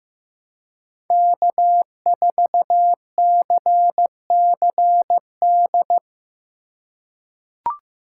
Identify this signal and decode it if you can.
Morse code, heard as a keyed tone, about 15 wpm, 700 Hz. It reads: K4CCD
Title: USA Call Signs - 5 Characters - 4x Character Spacing - 15wpm